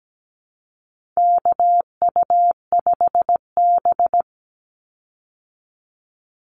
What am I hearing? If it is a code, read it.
Morse code, heard as a keyed tone, about 17 wpm, 700 Hz. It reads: KU5B
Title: Call Signs Repeated 2x - 17wpm